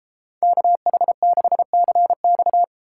Morse code, heard as a keyed tone, about 33 wpm, 700 Hz. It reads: KH6CX